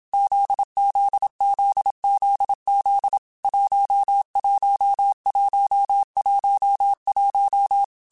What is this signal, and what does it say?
Morse code, heard as a keyed tone, about 27 wpm, 800 Hz. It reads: ZZZZZ 11111
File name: Groups@25 / Groups@25 mp3s / JZ17@25